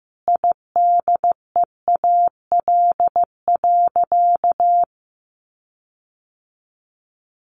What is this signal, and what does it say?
Morse code, heard as a keyed tone, about 15 wpm, 700 Hz. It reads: IDEAL.
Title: Sentences from Top 1400 Words 15wpm